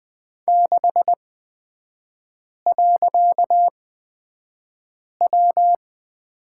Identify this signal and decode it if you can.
Morse code, heard as a keyed tone, about 20 wpm, spaced as for 5 wpm, 700 Hz. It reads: 6.W
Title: 255 Lesson 41e - Period - Three Characters 20wpm